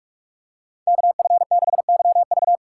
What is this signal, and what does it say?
Morse code, heard as a keyed tone, about 46 wpm, 700 Hz. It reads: KF6YV
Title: USA Call Signs - 5 Characters - 6x Character Spacing - 45wpm